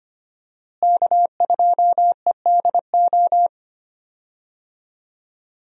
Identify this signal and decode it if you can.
Morse code, heard as a keyed tone, about 25 wpm, 700 Hz. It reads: K2EDO
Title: USA Call Signs - 5 Characters - Repeated 3x - 25wpm